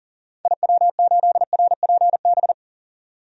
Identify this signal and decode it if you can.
Morse code, heard as a keyed tone, about 40 wpm, 700 Hz. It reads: IW8RPB
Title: DXCC-100 Call Signs - 6 Characters - 40wpm